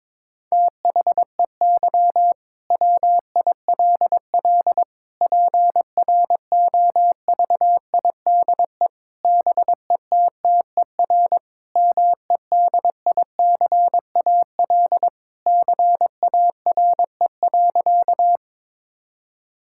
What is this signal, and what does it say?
Morse code, heard as a keyed tone, about 22 wpm, 700 Hz. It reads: THEY WILL PROVIDE BETTER MEDICAL CARE.
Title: Sentences from Top 1000 Words - Review Part 5 - 22wpm